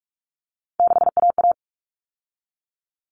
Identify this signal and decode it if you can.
Morse code, heard as a keyed tone, about 44 wpm, 700 Hz. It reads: 6AA
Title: Common Cut Numbers - 3-Digit 45wpm